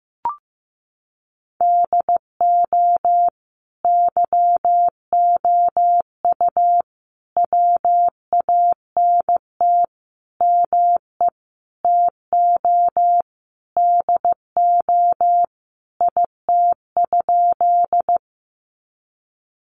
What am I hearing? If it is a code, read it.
Morse code, heard as a keyed tone, about 15 wpm, 700 Hz. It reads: DO YOU WANT ME TO DO IT?
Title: Sentences from Top 200 Words 15wpm